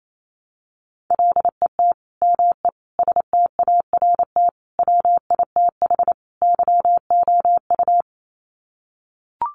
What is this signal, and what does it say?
Morse code, heard as a keyed tone, about 28 wpm, 700 Hz. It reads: LET ME START WITH YOU
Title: Sets of 5 Words Encore 28wpm